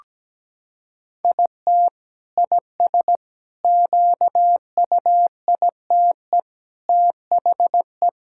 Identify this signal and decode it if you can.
Morse code, heard as a keyed tone, about 17 wpm, 700 Hz. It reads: IT IS QUITE THE
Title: Sentences from Top 1000 Words - Review Part 5 - 17wpm